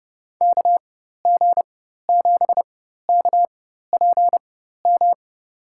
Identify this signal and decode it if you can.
Morse code, heard as a keyed tone, about 30 wpm, spaced as for 14 wpm, 700 Hz. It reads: KG7KPM
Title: USA Call Signs - 6 Characters - 4x Character Spacing - 30wpm